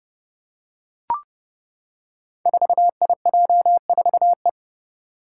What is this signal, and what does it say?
Morse code, heard as a keyed tone, about 30 wpm, 700 Hz. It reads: 4IJ4E
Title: Five Challenging Letter-Number Combinations 30wpm